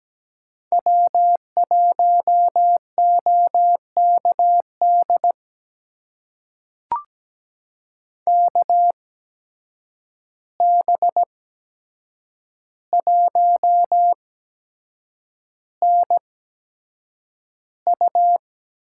Morse code, heard as a keyed tone, about 17 wpm, spaced as for 7 wpm, 700 Hz. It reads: W1OKD KB1NU
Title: USA Call Signs - 5 Characters - 8x Character Spacing - 17wpm